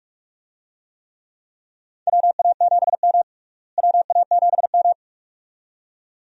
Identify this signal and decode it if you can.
Morse code, heard as a keyed tone, about 43 wpm, 700 Hz. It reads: WA7M WA7M
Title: USA Call Signs - 4 Characters - Repeated 2x - 45wpm